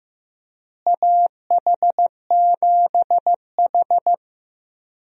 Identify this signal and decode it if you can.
Morse code, heard as a keyed tone, about 15 wpm, 700 Hz. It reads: AH7H
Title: USA Call Signs - 4 Characters - Repeated 3x - 15wpm